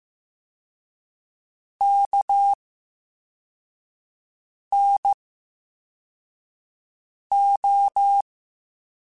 Morse code, heard as a keyed tone, about 15 wpm, 800 Hz. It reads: K N O
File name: Special INGs-mp3s / Special INGs-mp3s / VerySlowToFaster Ing @13